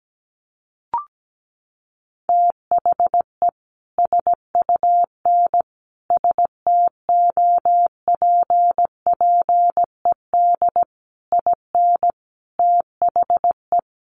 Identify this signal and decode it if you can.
Morse code, heard as a keyed tone, about 17 wpm, 700 Hz. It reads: THE SUN STOPPED IN THE
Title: A Sound Of Thunder 17wpm